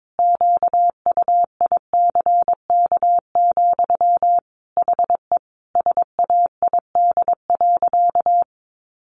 Morse code, heard as a keyed tone, about 22 wpm, 700 Hz. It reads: QUICK, HE SAID.
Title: Sentences from Top 1500 Words 22wpm